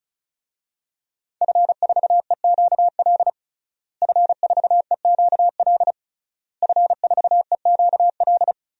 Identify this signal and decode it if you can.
Morse code, heard as a keyed tone, about 36 wpm, 690 Hz. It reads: F4EQL F4EQL F4EQL
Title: DXCC-100 Call Signs - 5 Characters - Repeated 3x - 35wpm